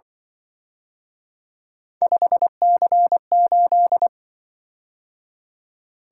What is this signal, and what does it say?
Morse code, heard as a keyed tone, about 25 wpm, 700 Hz. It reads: 5C8
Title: Speed-Racing - Three Letter-Number Combinations - 20wpm